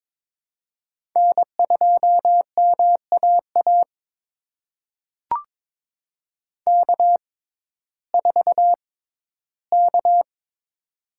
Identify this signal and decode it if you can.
Morse code, heard as a keyed tone, about 22 wpm, spaced as for 14 wpm, 700 Hz. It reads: N2MAA K4K
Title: USA Call Signs - 6x Character Spacing - 22wpm